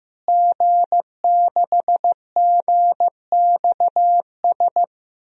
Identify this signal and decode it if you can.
Morse code, heard as a keyed tone, about 15 wpm, 700 Hz. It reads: G6GXS
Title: DXCC-100 Call Signs - 5 Characters - Repeated 2x - 15wpm